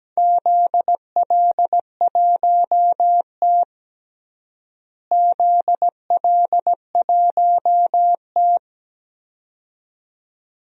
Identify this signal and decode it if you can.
Morse code, heard as a keyed tone, about 17 wpm, 700 Hz. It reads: ZL1T ZL1T
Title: DXCC-100 Call Signs - 4 Characters - Repeated 2x - 17wpm